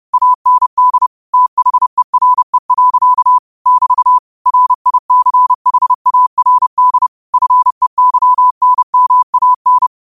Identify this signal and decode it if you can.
Morse code, heard as a keyed tone, about 30 wpm, 1000 Hz. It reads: AND THERE. <BT> RICHARD FEYNMAN
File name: qotdfiles / 30wpm